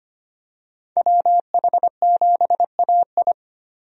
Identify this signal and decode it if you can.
Morse code, heard as a keyed tone, about 25 wpm, 700 Hz. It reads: WH7AI
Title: USA Call Signs - 5 Characters - Repeated 2x - 25wpm